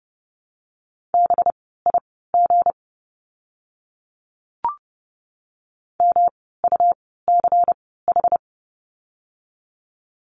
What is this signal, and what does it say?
Morse code, heard as a keyed tone, about 30 wpm, spaced as for 17 wpm, 700 Hz. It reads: BIG MUCH
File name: course / 30:10wpm / 237 Lesson 38f - Letter q - Words 30wpm